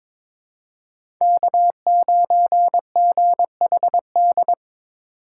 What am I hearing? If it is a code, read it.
Morse code, heard as a keyed tone, about 23 wpm, 700 Hz. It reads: K9GHD
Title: USA Call Signs - 5 Characters - 8x Character Spacing - 22wpm